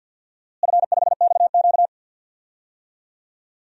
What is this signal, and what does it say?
Morse code, heard as a keyed tone, about 52 wpm, 700 Hz. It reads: F5XQ
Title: DXCC-100 Call Signs - 4 Characters - Repeated 3x - 50wpm